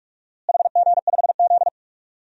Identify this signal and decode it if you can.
Morse code, heard as a keyed tone, about 41 wpm, 700 Hz. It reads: HG5Z
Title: DXCC-100 Call Signs - 4 Characters - Repeated 2x - 45wpm